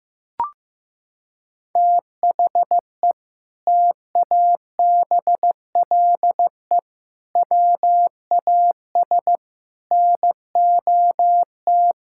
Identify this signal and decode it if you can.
Morse code, heard as a keyed tone, about 15 wpm, 700 Hz. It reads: THE TABLE WAS NOT
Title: Sentences from Top 1000 Words - Review Part 6 - 15wpm